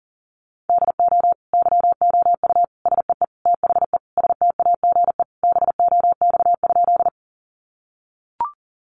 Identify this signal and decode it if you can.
Morse code, heard as a keyed tone, about 40 wpm, 700 Hz. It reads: DO YOU SEE THE STAGE BOX?
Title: Sentences from Top 1200 Words 40wpm